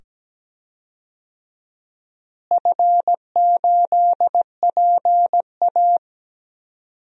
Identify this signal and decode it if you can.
Morse code, heard as a keyed tone, about 17 wpm, 700 Hz. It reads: F8PA
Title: DXCC-100 Call Signs - 4 Characters - Repeated 2x - 17wpm